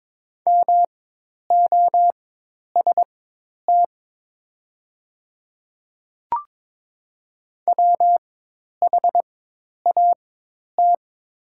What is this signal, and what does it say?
Morse code, heard as a keyed tone, about 22 wpm, spaced as for 10 wpm, 700 Hz. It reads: MOST WHAT
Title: 4x Character Spacing - Top 100 Words - 22wpm